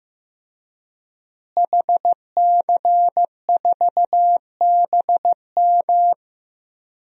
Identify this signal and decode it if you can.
Morse code, heard as a keyed tone, about 15 wpm, 700 Hz. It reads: HC4BM
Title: DXCC-100 Call Signs - 5 Characters - Repeated 3x - 15wpm